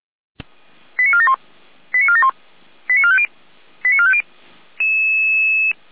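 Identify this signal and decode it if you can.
Five-tone selective call (ZVEI), heard as quick runs of five tones, tones of about 70 ms. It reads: 89471 89471 89450 89450
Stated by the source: ZVEI five-tone selective call received on SDR.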